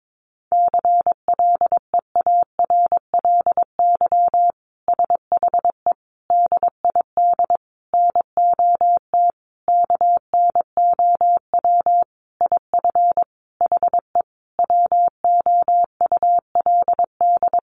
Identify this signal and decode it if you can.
Morse code, heard as a keyed tone, about 22 wpm, 700 Hz. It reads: CLEARLY SHE DID NOT KNOW IF HE WOULD
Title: Sentences from Top 1300 Words 22wpm